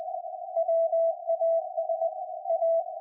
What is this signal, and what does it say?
Morse code, heard as a keyed tone, about 20 wpm, 660 Hz. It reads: WAS A
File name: IC-7100